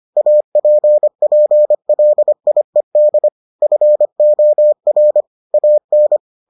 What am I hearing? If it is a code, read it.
Morse code, heard as a keyed tone, about 25 wpm, 600 Hz. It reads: APPLIED FOR AN